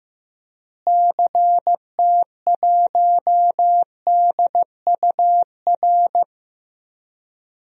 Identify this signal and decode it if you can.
Morse code, heard as a keyed tone, about 15 wpm, 700 Hz. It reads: CT1DUR
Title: DXCC-100 Call Signs - 6 Characters - Repeated 3x - 15wpm